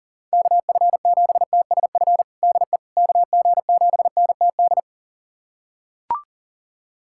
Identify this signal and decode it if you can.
Morse code, heard as a keyed tone, about 40 wpm, 700 Hz. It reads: KF7TSF DE KG7NTD
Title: USA Callsign DE Callsign - 6 Characters - 40wpm